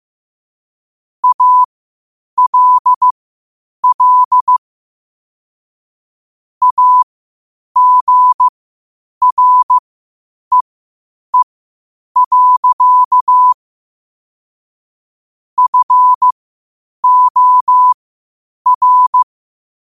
Morse code, heard as a keyed tone, about 15 wpm, spaced as for 8 wpm, 1000 Hz. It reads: ALL AGREE. FOR
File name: qotdfiles / 8wpm